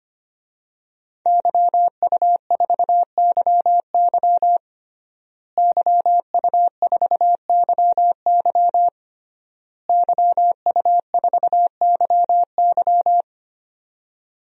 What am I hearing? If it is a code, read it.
Morse code, heard as a keyed tone, about 25 wpm, 700 Hz. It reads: YU4YY YU4YY YU4YY
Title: DXCC-100 Call Signs - 5 Characters - Repeated 3x - 25wpm